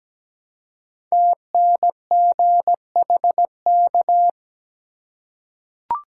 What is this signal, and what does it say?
Morse code, heard as a keyed tone, about 17 wpm, 700 Hz. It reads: TNGHK
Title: Code Groups - 5 Letter:Numbers - 17wpm